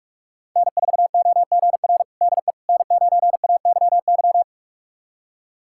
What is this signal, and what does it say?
Morse code, heard as a keyed tone, about 46 wpm, 700 Hz. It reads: N4OGR DE N9AYY
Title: USA Callsign DE Callsign - 5 Characters - 45wpm